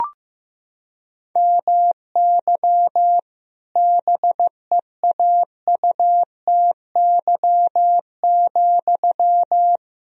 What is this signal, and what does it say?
Morse code, heard as a keyed tone, about 15 wpm, 700 Hz. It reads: MY BEAUTY,
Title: Sentences from Top 1000 Words - Review Part 2 - 15wpm